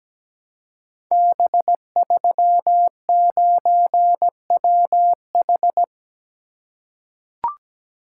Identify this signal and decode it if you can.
Morse code, heard as a keyed tone, about 17 wpm, 700 Hz. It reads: B39WH
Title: Code Groups - 5 Letter:Numbers - 17wpm